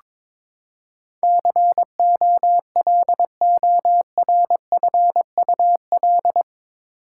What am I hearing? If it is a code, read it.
Morse code, heard as a keyed tone, about 22 wpm, 700 Hz. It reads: COLORFUL
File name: practice / 22wpm / Eight Letter Words - 22wpm